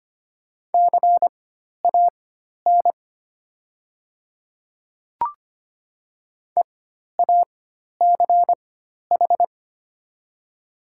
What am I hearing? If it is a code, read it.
Morse code, heard as a keyed tone, about 25 wpm, spaced as for 12 wpm, 700 Hz. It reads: CAN EACH
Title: 4x Character Spacing - Top 500 Words - 25wpm